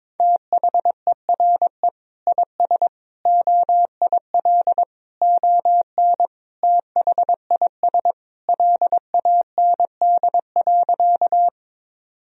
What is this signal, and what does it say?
Morse code, heard as a keyed tone, about 22 wpm, 700 Hz. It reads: THERE IS OIL ON THIS LAND.
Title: Sentences from Top 1000 Words - Review Part 4 - 22wpm